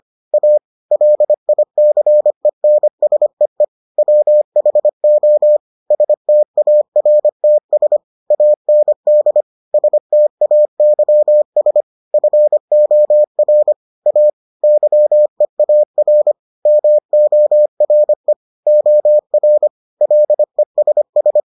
Morse code, heard as a keyed tone, about 25 wpm, 600 Hz. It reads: A LICENSEE WHO STARTS AND STAYS FOR A YEAR MORE OR LESS